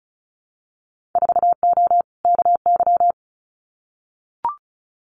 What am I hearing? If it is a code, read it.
Morse code, heard as a keyed tone, about 36 wpm, 700 Hz. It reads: 4O KY
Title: Winter Field Day Exchanges Encore 35wpm